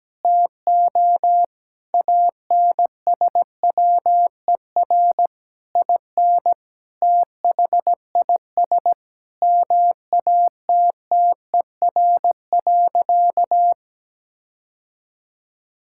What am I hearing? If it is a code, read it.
Morse code, heard as a keyed tone, about 17 wpm, 700 Hz. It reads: TO ANSWER IN THIS MATTER.
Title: Sentences from Top 1600 Words 17wpm